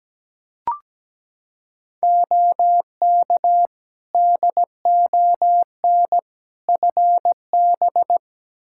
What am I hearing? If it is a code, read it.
Morse code, heard as a keyed tone, about 17 wpm, 700 Hz. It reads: OK DON FB
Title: Practice QSO - Exchange 2 - Info - 17wpm